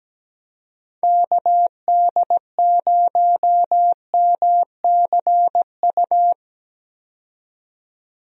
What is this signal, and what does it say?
Morse code, heard as a keyed tone, about 17 wpm, 700 Hz. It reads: KD0MCU
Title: USA Call Signs - 6 Characters - 17wpm